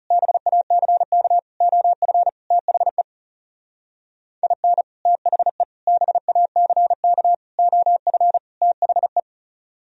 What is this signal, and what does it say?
Morse code, heard as a keyed tone, about 36 wpm, 700 Hz. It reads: BACK OF THE IN THE BACK OF THE
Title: Speed-Racing - Sets of 5 Words 30wpm